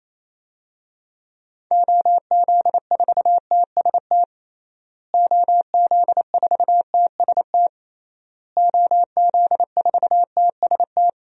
Morse code, heard as a keyed tone, about 27 wpm, 700 Hz. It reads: OZ4TST OZ4TST OZ4TST
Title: DXCC-100 Call Signs - 6 Characters - Repeated 3x - 28wpm